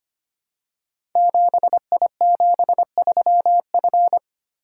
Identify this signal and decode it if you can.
Morse code, heard as a keyed tone, about 25 wpm, 700 Hz. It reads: 7I73F
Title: Code Groups - 5 Letter:Numbers - 25wpm